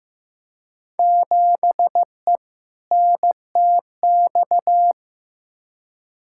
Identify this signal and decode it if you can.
Morse code, heard as a keyed tone, about 15 wpm, 700 Hz. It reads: 7E NTX
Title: Field Day Exchanges Encore - 15wpm